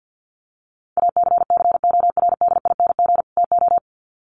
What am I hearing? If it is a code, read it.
Morse code, heard as a keyed tone, about 50 wpm, 700 Hz. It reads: ACCORDING TO
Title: Sets of 2 Words 50wpm